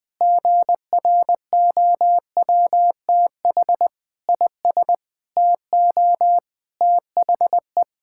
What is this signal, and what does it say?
Morse code, heard as a keyed tone, about 20 wpm, 700 Hz. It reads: GROWTH IS TO THE
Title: Sentences from Top 1000 Words - Review Part 5 - 20wpm